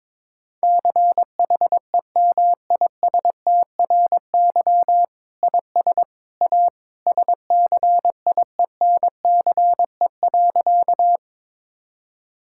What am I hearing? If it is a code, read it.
Morse code, heard as a keyed tone, about 22 wpm, 700 Hz. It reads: CHEMISTRY IS A SCIENCE.
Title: Sentences from Top 1000 Words - Review Part 6 - 22wpm